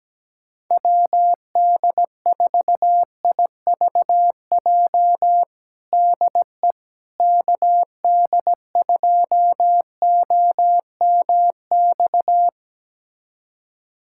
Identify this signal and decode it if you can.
Morse code, heard as a keyed tone, about 17 wpm, 700 Hz. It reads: WD4IVJ DE KD2OMX
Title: USA Callsign DE Callsign 17wpm